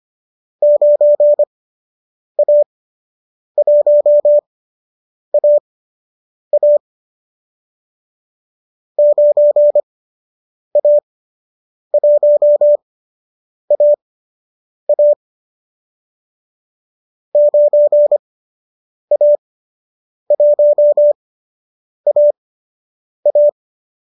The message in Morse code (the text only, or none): 9A1AA 9A1AA 9A1AA